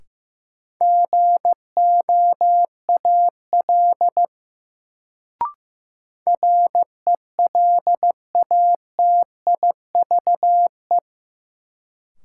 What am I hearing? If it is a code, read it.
Morse code, heard as a keyed tone, about 15 wpm, 700 Hz. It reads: GOAL RELATIVE
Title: Sentences from Top 1400 Words - New Word Drill Alt - 15wpm